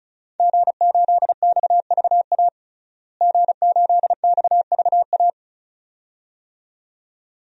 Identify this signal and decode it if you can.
Morse code, heard as a keyed tone, about 36 wpm, 700 Hz. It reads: G8XVA G8XVA
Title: DXCC-100 Call Signs - 5 Characters - Repeated 2x - 35wpm